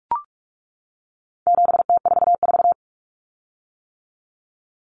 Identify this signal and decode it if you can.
Morse code, heard as a keyed tone, about 44 wpm, 700 Hz. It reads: 7T44
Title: Common Cut Numbers - 4-Digit 45wpm